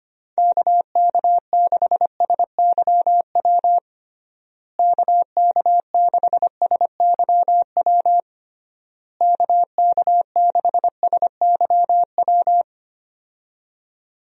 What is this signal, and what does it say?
Morse code, heard as a keyed tone, about 25 wpm, 700 Hz. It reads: KK6SYW KK6SYW KK6SYW
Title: USA Call Signs - 6 Characters - Repeated 3x - 25wpm